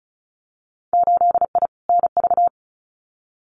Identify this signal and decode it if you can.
Morse code, heard as a keyed tone, about 34 wpm, 700 Hz. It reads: 8I NV
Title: Winter Field Day Exchanges Encore 35wpm